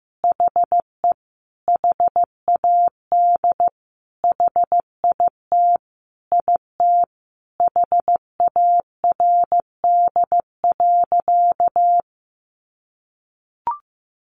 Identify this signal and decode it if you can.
Morse code, heard as a keyed tone, about 15 wpm, 700 Hz. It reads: HE HAD HIT IT HARD.
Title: Sentences from Top 1200 Words 15wpm